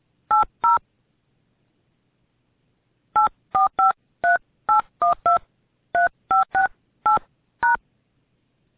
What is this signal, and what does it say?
DTMF, heard as telephone touch tones, tones of about 130 ms. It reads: 80 8453812 3568#